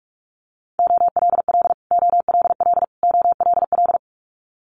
Morse code, heard as a keyed tone, about 46 wpm, 700 Hz. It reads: OLL OLL OLL
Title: Multi-letter Phonemes Repeated 3x 45wpm